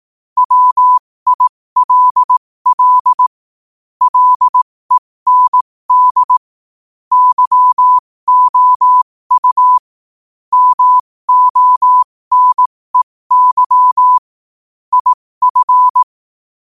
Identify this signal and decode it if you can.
Morse code, heard as a keyed tone, about 18 wpm, spaced as for 15 wpm, 1000 Hz. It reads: WILL LEND YOU MONEY IF
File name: qotdfiles / 15wpm